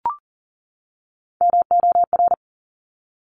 Morse code, heard as a keyed tone, about 40 wpm, 700 Hz. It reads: MOR
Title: Top Three Letter Combinations in Words 40wpm